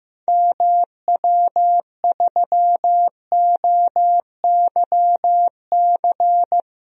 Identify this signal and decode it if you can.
Morse code, heard as a keyed tone, about 15 wpm, 700 Hz. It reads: MW3OYC
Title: DXCC-100 Call Signs - 6 Characters - Repeated 2x - 15wpm